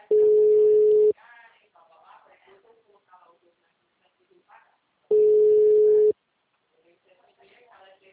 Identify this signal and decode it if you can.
Ringback tone (Europe, 425 Hz), with speech.